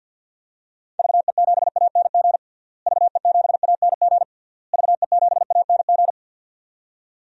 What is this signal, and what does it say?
Morse code, heard as a keyed tone, about 50 wpm, 700 Hz. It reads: VE7ANG VE7ANG VE7ANG
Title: DXCC-100 Call Signs - 6 Characters - Repeated 3x - 50wpm